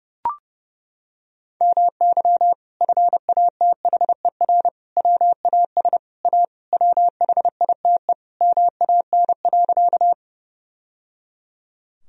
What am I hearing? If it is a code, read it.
Morse code, heard as a keyed tone, about 30 wpm, 700 Hz. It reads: MY FATHER WAS A WHITE MAN.
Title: Sentences from Top 1000 Words - Review Part 1 - 30wpm